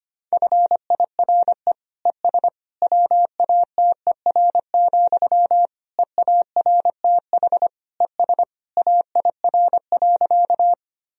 Morse code, heard as a keyed tone, about 25 wpm, 700 Hz. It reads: FIRE ES WATER, EARTH ES AIR.
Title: Sentences from Top 400 Words and Abbreviations 25wpm